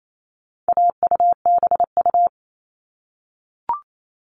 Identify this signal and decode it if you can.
Morse code, heard as a keyed tone, about 28 wpm, 700 Hz. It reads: AUBU
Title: All Cut Numbers - 4-Digit 28wpm